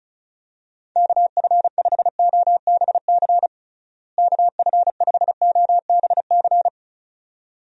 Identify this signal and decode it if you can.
Morse code, heard as a keyed tone, about 34 wpm, 700 Hz. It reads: KF5OBC KF5OBC
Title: USA Call Signs - 6 Characters - Repeated 2x - 35wpm